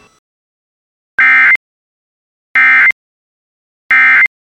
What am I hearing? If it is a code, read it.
Emergency Alert System (SAME) tones, heard as harsh buzzing bursts. It is heard from 1.2 to 4.3 s.